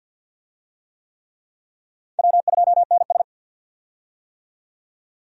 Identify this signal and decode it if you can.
Morse code, heard as a keyed tone, about 47 wpm, 700 Hz. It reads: W2NS
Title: USA Call Signs - 4 Characters - 50wpm